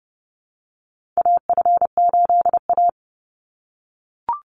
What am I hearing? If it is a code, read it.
Morse code, heard as a keyed tone, about 30 wpm, 700 Hz. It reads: AF8A
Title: Call Signs - 30wpm